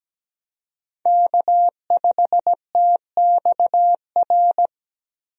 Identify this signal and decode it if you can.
Morse code, heard as a keyed tone, about 17 wpm, 700 Hz. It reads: K5TXR
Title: USA Call Signs - 5 Characters - Repeated 2x - 17wpm